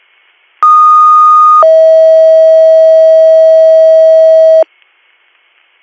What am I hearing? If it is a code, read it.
Two-tone page, heard as a one-second tone, then a three-second tone, about 1230 then 640 Hz. It is heard from 0.6 to 4.6 s.